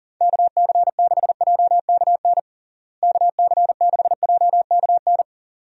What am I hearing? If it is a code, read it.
Morse code, heard as a keyed tone, about 40 wpm, 700 Hz. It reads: KC6JKN KC6JKN
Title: USA Call Signs - 6 Characters - Repeated 2x - 40wpm